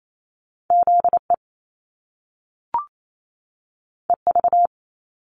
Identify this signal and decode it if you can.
Morse code, heard as a keyed tone, about 28 wpm, 700 Hz. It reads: ZE EV
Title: Top Two Letter Combinations in Words 28wpm